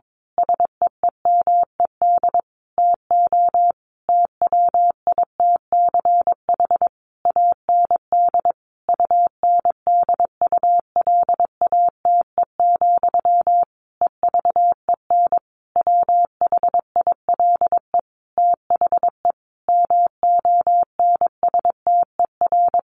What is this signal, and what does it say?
Morse code, heard as a keyed tone, about 22 wpm, 700 Hz. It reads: SEEMED TO TWITCH AND UNDULATE, EVEN WHILE THE MONSTER
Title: A Sound Of Thunder 22wpm